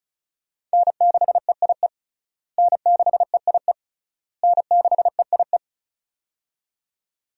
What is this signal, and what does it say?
Morse code, heard as a keyed tone, about 36 wpm, 700 Hz. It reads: N6EIE N6EIE N6EIE
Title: USA Call Signs - 5 Characters - Repeated 3x - 35wpm